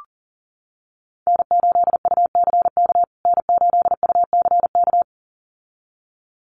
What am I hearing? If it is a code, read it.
Morse code, heard as a keyed tone, about 40 wpm, 700 Hz. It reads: N8UCK N8UCK
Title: USA Call Signs Repeated 2x 40wpm